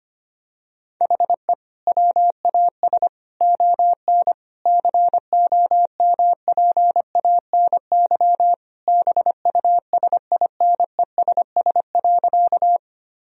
Morse code, heard as a keyed tone, about 25 wpm, 700 Hz. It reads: HE WAS ON COMPANY BUSINESS.
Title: Sentences from Top 1000 Words - Review Part 3 - 25wpm